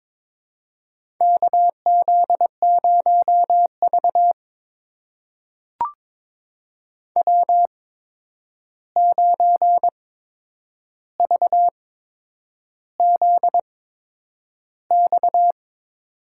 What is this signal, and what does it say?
Morse code, heard as a keyed tone, about 22 wpm, spaced as for 9 wpm, 700 Hz. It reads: KZ0V W9VZX
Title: USA Call Signs - 8x Character Spacing - 22wpm